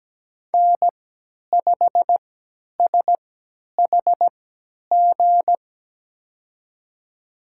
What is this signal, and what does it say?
Morse code, heard as a keyed tone, about 17 wpm, spaced as for 10 wpm, 700 Hz. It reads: N5SHG
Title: USA Call Signs - 3x Character Spacing - 17wpm